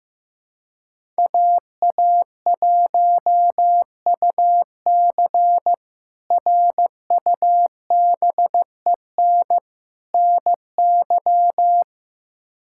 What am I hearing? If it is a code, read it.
Morse code, heard as a keyed tone, about 15 wpm, 700 Hz. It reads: AA1UC RUBEN NY
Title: K1USN SST QSO Practice - 15wpm